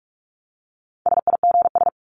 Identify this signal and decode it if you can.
Morse code, heard as a keyed tone, about 44 wpm, 700 Hz. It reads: SIGS
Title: Abbreviations 45wpm